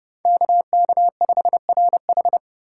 Morse code, heard as a keyed tone, about 31 wpm, 700 Hz. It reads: KK5RH